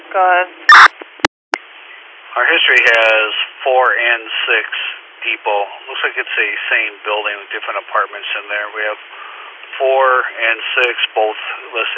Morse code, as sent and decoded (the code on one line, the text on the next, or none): none